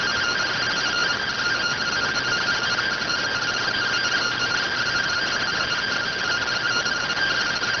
Two-tone teleprinter signal, about 170 Hz shift.